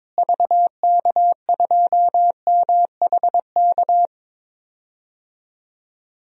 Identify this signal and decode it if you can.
Morse code, heard as a keyed tone, about 22 wpm, 700 Hz. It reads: VK2MHK
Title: DXCC-100 Call Signs - 6 Characters - Repeated 2x - 22wpm